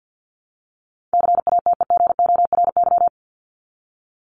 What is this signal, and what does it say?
Morse code, heard as a keyed tone, about 50 wpm, 700 Hz. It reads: CATEGORY
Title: Top 751-1000 Words 50wpm